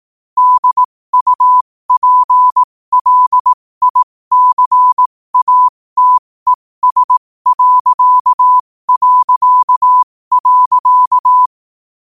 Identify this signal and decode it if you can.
Morse code, heard as a keyed tone, about 18 wpm, spaced as for 15 wpm, 1000 Hz. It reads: DUPLICATES...
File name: qotdfiles / 15wpm